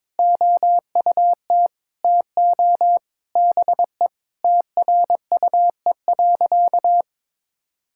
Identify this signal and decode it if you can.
Morse code, heard as a keyed tone, about 22 wpm, 700 Hz. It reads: OUT TO BE TRUE.
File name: practice / 22wpm / Sentences with 3-Word Phrases - 22wpm